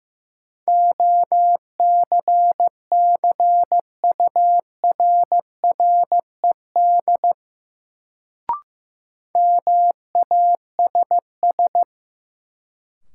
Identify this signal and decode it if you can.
Morse code, heard as a keyed tone, about 16 wpm, 700 Hz. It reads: OCCURRED MASS
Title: Sets of 5 Words Encore - New Word Drill - 15wpm